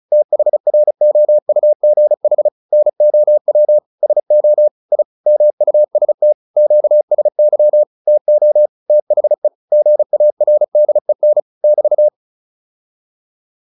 Morse code, heard as a keyed tone, about 35 wpm, 600 Hz. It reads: THROUGH NOW SO I MUST QSY TO THE GARDEN <BT>